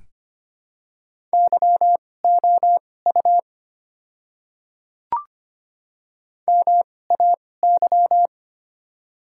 Morse code, heard as a keyed tone, about 25 wpm, spaced as for 18 wpm, 700 Hz. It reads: YOU MAY
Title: 2x Character Spacing - Top 500 Words - 25wpm